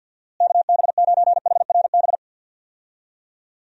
Morse code, heard as a keyed tone, about 51 wpm, 700 Hz. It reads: KB9HRB
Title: USA Call Signs - 6 Characters - Repeated 2x - 50wpm